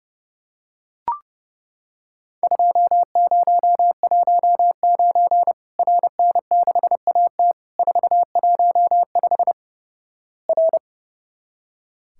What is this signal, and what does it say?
Morse code, heard as a keyed tone, about 32 wpm, 700 Hz. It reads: 2019 RN6AT 415 R
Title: Worked All Europe DX Contest - QTC Exchange - 30wpm